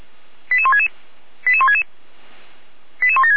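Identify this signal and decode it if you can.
Five-tone selective call (ZVEI), heard as quick runs of five tones, tones of about 80 ms. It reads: 88166 88166 88176
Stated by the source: ZVEI five-tone selective call received on an SDR by the Feuerwehr Orlamünde fire brigade.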